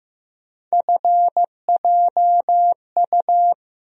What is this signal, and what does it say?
Morse code, heard as a keyed tone, about 15 wpm, 700 Hz. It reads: FJU
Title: USA Call Sign Postfixes - 3 Letters 15wpm